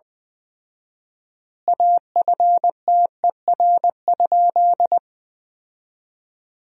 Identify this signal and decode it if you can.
Morse code, heard as a keyed tone, about 20 wpm, 700 Hz. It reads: AFTER?
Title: 4x Word-Spacing - Sentences from Top 200 Words - 20wpm